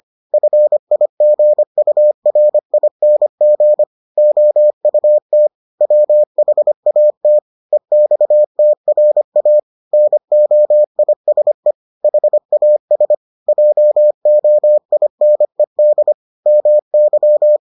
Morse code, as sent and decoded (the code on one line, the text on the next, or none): ..-. .. --. ..- .-. .. -. --. / --- ..- - / .-- .... .- - / . -..- - .-. .- / -. --- .. ... / .... .- ... / .--- --- .. -. -.. / -- -.--
FIGURING OUT WHAT EXTRA NOISE HAS JOINED MY